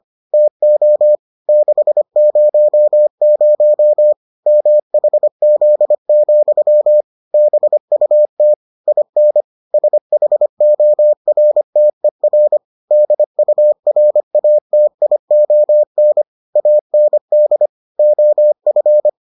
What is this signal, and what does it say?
Morse code, heard as a keyed tone, about 25 wpm, 600 Hz. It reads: TO 600 MHZ, BUT IN SHORTER DURATION AND OF